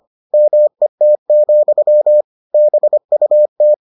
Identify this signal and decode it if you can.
Morse code, heard as a keyed tone, about 25 wpm, 600 Hz. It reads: MET, BUT